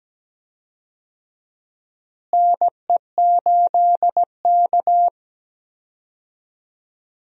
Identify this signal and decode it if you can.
Morse code, heard as a keyed tone, about 17 wpm, 700 Hz. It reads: NE8K
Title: USA Call Signs - 4 Characters - Repeated 3x - 17wpm